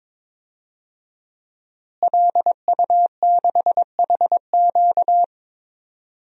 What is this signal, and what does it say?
Morse code, heard as a keyed tone, about 22 wpm, 700 Hz. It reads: LU6HQ
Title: DXCC-100 Call Signs - 5 Characters - Repeated 2x - 22wpm